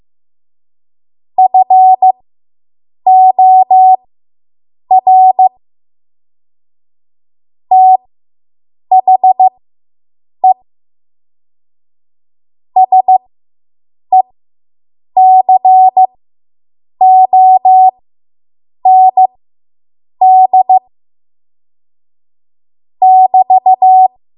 Morse code, heard as a keyed tone, about 15 wpm, spaced as for 7 wpm, 750 Hz. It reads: FOR THE SECOND <BT>